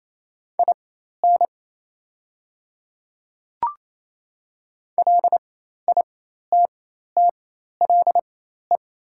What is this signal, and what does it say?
Morse code, heard as a keyed tone, about 28 wpm, spaced as for 13 wpm, 700 Hz. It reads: IN LITTLE
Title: 4x Character Spacing - Top 200 Words - 28wpm